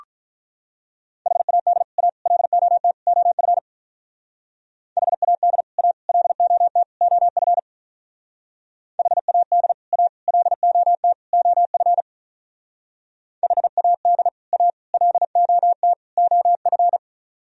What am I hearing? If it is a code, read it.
Morse code, heard as a keyed tone, about 44 wpm, 700 Hz. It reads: HAD A LOT OF HAD A LOT OF HAD A LOT OF HAD A LOT OF
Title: Speed-Racing - Sets of 4 Words 35wpm